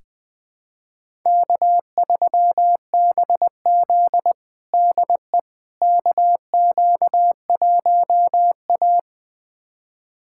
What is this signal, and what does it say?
Morse code, heard as a keyed tone, about 20 wpm, 700 Hz. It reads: K3BZ DE KQ1A
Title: USA Callsign DE Callsign - 4 Characters - 20wpm